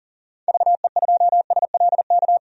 Morse code, heard as a keyed tone, about 40 wpm, 700 Hz. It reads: VE2SLK
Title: DXCC-100 Call Signs - 6 Characters - 40wpm